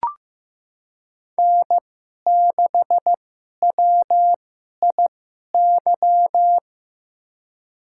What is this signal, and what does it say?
Morse code, heard as a keyed tone, about 15 wpm, spaced as for 11 wpm, 700 Hz. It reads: N6WIY
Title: USA Call Signs - 5 Characters - 2x Character Spacing - 15wpm